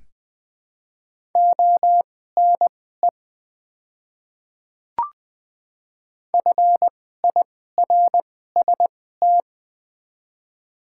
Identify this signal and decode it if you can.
Morse code, heard as a keyed tone, about 21 wpm, spaced as for 15 wpm, 700 Hz. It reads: ONE FIRST
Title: 2x Character Spacing - Top 100 Words - 20wpm